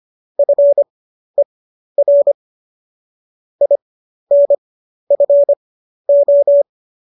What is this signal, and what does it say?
Morse code, heard as a keyed tone, about 25 wpm, spaced as for 12 wpm, 570 Hz. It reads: FER INFO